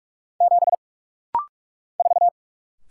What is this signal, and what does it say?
Morse code, heard as a keyed tone, about 44 wpm, 700 Hz. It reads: 7 4
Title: Single Letter-Number Alt - Rapid-Fire 45wpm